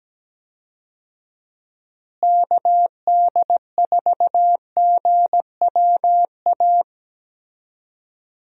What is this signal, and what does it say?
Morse code, heard as a keyed tone, about 17 wpm, 700 Hz. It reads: KD4GWA
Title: USA Call Signs - 6 Characters - Repeated 2x - 17wpm